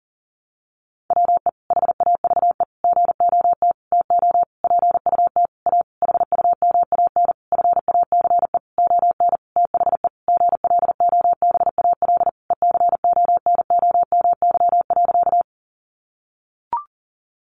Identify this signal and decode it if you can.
Morse code, heard as a keyed tone, about 40 wpm, 700 Hz. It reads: WE HAVE GOT TO PUT A HUMAN FACE ON THE GLOBAL ECONOMY.
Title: Sentences from Top 1200 Words 40wpm